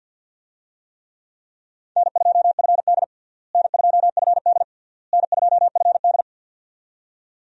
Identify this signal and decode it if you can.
Morse code, heard as a keyed tone, about 49 wpm, 700 Hz. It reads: N2FD N2FD N2FD
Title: USA Call Signs - 4 Characters - Repeated 3x - 50wpm